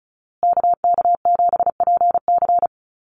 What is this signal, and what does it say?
Morse code, heard as a keyed tone, about 34 wpm, 700 Hz. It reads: KK7PC
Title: USA Call Signs Repeated 3x 35wpm